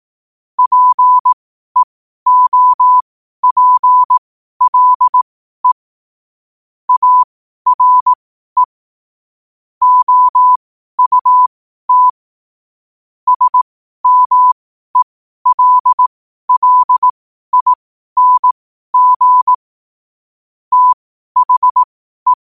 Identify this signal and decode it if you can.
Morse code, heard as a keyed tone, about 18 wpm, spaced as for 12 wpm, 1000 Hz. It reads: PEOPLE ARE OUT SMELLING THE
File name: qotdfiles / 12wpm